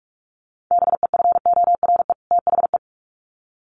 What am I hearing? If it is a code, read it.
Morse code, heard as a keyed tone, about 44 wpm, 700 Hz. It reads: BEFORE THE
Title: Sets of 2 Words 45wpm